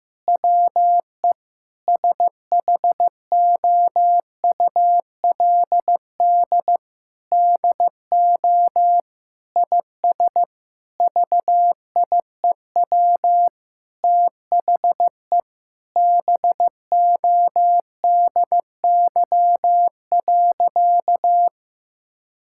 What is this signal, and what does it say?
Morse code, heard as a keyed tone, about 15 wpm, 700 Hz. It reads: WE SHOULD DO IS VIEW THE BODY.
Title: Sentences from Top 1000 Words - Review Part 6 - 15wpm